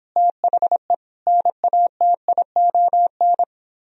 Morse code, heard as a keyed tone, about 26 wpm, 700 Hz. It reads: THE NATION